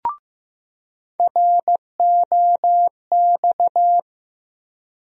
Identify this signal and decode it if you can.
Morse code, heard as a keyed tone, about 15 wpm, 700 Hz. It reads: ROX
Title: USA Call Sign Postfixes - 3 Letters 15wpm